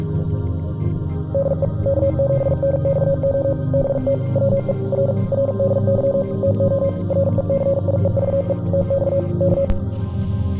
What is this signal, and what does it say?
Morse code, heard as a keyed tone, about 46 wpm, 550 Hz. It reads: DE K7NXO BT WE R RDY TO RECIEVE TX K